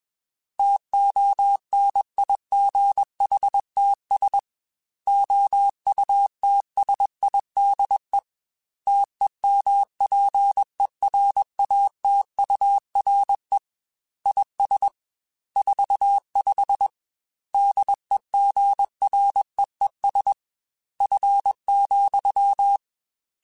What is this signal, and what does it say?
Morse code, heard as a keyed tone, about 21 wpm, 800 Hz. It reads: TONIGHTS OUTSIDE TEMPERATURE IS 45 DEGREES F,
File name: Weather Sets / Weather Sets-mp3s / Weather 3 17 @20